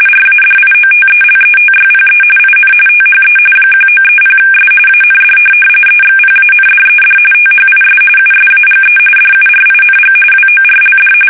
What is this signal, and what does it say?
RTTY, about 45.45 baud. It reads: RADIOTELETYPE IN THE 1930S AND EXPANDED THIS USAGE DURING WORLD
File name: fsk850 / teletype-RTTY